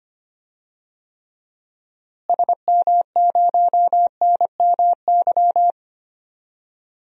Morse code, heard as a keyed tone, about 25 wpm, 700 Hz. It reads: SM0NMY
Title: DXCC-100 Call Signs - 6 Characters - Repeated 2x - 25wpm